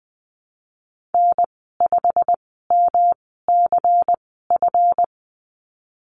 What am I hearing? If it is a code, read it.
Morse code, heard as a keyed tone, about 20 wpm, spaced as for 15 wpm, 700 Hz. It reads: N5MCF
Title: 147 Lesson 24g - Letter y - Call Signs 20wpm